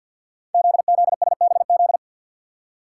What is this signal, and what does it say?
Morse code, heard as a keyed tone, about 48 wpm, 700 Hz. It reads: ZZSBZ
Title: Code Groups - 5 Letters - 50wpm